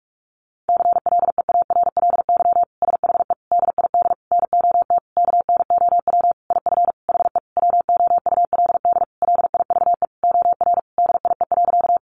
Morse code, heard as a keyed tone, about 44 wpm, 700 Hz. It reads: CLEARLY SHE DID NOT KNOW IF HE WOULD LIVE OR DIE.
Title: Sentences from Top 1300 Words 45wpm